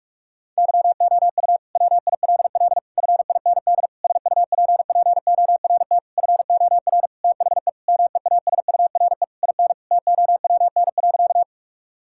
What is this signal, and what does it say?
Morse code, heard as a keyed tone, about 46 wpm, 700 Hz. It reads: YOU WILL FIND SUPPORT FOR THE MEASURE IN TOWN.
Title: Sentences from Top 1000 Words - Review Part 5 - 45wpm